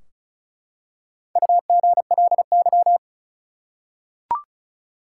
Morse code, heard as a keyed tone, about 36 wpm, 700 Hz. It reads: UGLY
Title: Four Letter Words - 35wpm